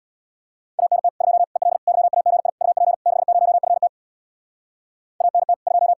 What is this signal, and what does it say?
Morse code, heard as a keyed tone, about 65 wpm, 700 Hz. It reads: ANT 2 EL YAGI UP 60FT ANT 2